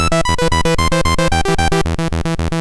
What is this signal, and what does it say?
There is no code.